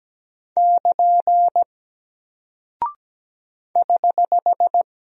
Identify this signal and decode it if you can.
Morse code, heard as a keyed tone, about 17 wpm, 700 Hz. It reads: <KN> <HH>
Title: Single Character (Letter:Number:Punctuation:Prosign) 17wpm